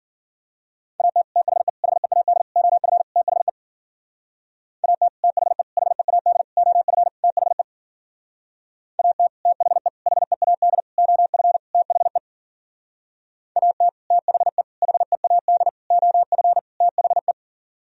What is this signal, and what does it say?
Morse code, heard as a keyed tone, about 52 wpm, 700 Hz. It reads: AT THE HEAD OF THE AT THE HEAD OF THE AT THE HEAD OF THE AT THE HEAD OF THE